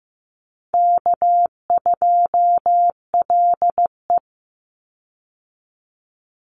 Morse code, heard as a keyed tone, about 15 wpm, 700 Hz. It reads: K2LE
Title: Easy Call Signs - 15wpm